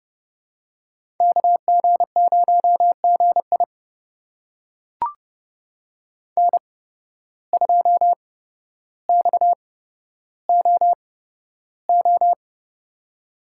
Morse code, heard as a keyed tone, about 32 wpm, spaced as for 13 wpm, 700 Hz. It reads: KG0GI N2XOO
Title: USA Call Signs - 5 Characters - 8x Character Spacing - 30wpm